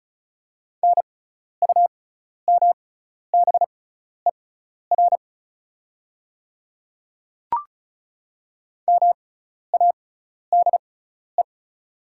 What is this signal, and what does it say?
Morse code, heard as a keyed tone, about 34 wpm, spaced as for 12 wpm, 700 Hz. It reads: NUMBER MADE
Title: 6x Character Spacing - Top 200 Words - 35wpm